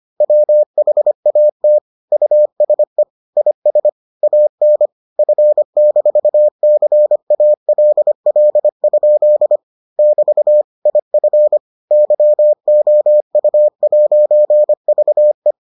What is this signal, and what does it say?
Morse code, heard as a keyed tone, about 25 wpm, 600 Hz. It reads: WHAT USE IS AN F-CALL? <BT> IF YOU'VE